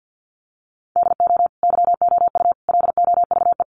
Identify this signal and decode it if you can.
Morse code, heard as a keyed tone, about 50 wpm, 700 Hz. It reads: DO YOU LOVE
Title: Sentences from Top 400 Words 50wpm